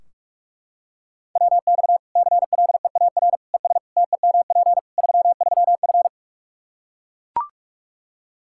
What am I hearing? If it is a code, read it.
Morse code, heard as a keyed tone, about 41 wpm, 700 Hz. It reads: WX CLEAR ES TEMP 33F
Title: Practice QSO - Exchange 2 - Info - 45wpm